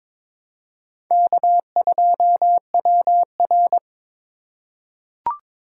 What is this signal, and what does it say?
Morse code, heard as a keyed tone, about 23 wpm, 700 Hz. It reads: K2WR
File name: practice / 22wpm / Four Letter-Number Combinations - 22wpm